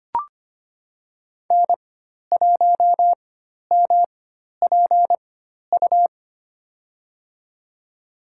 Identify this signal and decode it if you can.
Morse code, heard as a keyed tone, about 25 wpm, spaced as for 12 wpm, 700 Hz. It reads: N1MPU
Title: USA Call Signs - 5 Characters - 4x Character Spacing - 25wpm